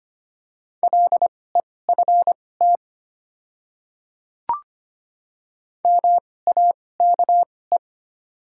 Morse code, heard as a keyed tone, about 25 wpm, spaced as for 18 wpm, 700 Hz. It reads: LEFT MAKE
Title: 2x Character Spacing - Top 300 Words - 25wpm